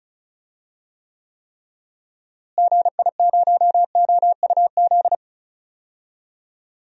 Morse code, heard as a keyed tone, about 36 wpm, 700 Hz. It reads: GI0OUZ
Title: DXCC-100 Call Signs - 6 Characters - Repeated 3x - 35wpm